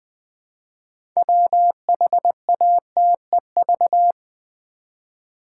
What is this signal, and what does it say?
Morse code, heard as a keyed tone, about 20 wpm, 700 Hz. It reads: WHATEV
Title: Build A Word - 2 Letters At A Time - 8-Letter Words - 20wpm